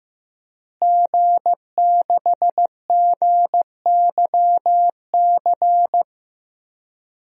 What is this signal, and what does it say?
Morse code, heard as a keyed tone, about 15 wpm, 700 Hz. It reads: G6GYC
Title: DXCC-100 Call Signs - 5 Characters - Repeated 2x - 15wpm